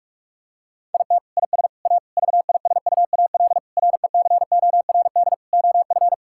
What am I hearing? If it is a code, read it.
Morse code, heard as a keyed tone, about 41 wpm, 700 Hz. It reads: IT IS A VISUAL RECORD OF
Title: Sentences from Top 1000 Words - Review Part 6 - 45wpm